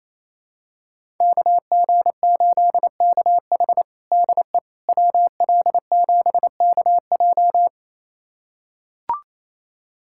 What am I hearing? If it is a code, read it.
Morse code, heard as a keyed tone, about 27 wpm, 700 Hz. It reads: KG8KH DE WL7KJ
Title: USA Callsign DE Callsign - 5 Characters - 28wpm